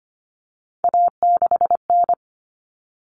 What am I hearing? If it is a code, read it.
Morse code, heard as a keyed tone, about 25 wpm, 700 Hz. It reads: A6N
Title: All Cut Numbers - 3-Digit 25wpm